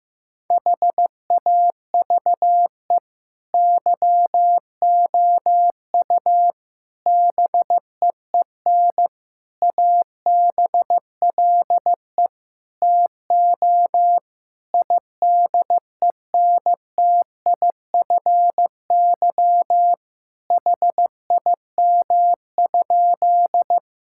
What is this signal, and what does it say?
Morse code, heard as a keyed tone, about 15 wpm, 700 Hz. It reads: HAVE YOU BEEN ABLE TO IDENTIFY HIM?
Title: Sentences from Top 1000 Words - Review Part 6 - 15wpm